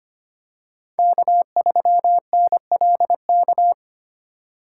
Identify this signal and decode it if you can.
Morse code, heard as a keyed tone, about 25 wpm, 700 Hz. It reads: K3NLK
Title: USA Call Signs - 5 Characters - 4x Character Spacing - 25wpm